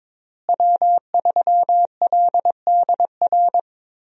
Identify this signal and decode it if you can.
Morse code, heard as a keyed tone, about 22 wpm, 700 Hz. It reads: W3LDR
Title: USA Call Signs - 5 Characters - Repeated 3x - 22wpm